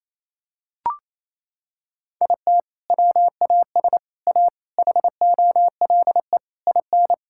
Morse code, heard as a keyed tone, about 28 wpm, 700 Hz. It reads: IT WAS A HOLE IN
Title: Sentences from Top 1600 Words 28wpm